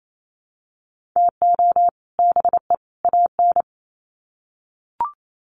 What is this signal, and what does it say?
Morse code, heard as a keyed tone, about 28 wpm, 700 Hz. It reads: TO BE AN
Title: Sets of 3 Words 28wpm